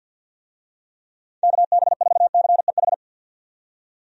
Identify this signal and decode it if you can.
Morse code, heard as a keyed tone, about 50 wpm, 700 Hz. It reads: KB4CEH
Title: USA Call Signs - 6 Characters - 50wpm